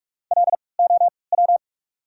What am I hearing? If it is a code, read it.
Morse code, heard as a keyed tone, about 46 wpm, 700 Hz. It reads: L O W
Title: 3x Character Spacing - Top 500 Words - 45wpm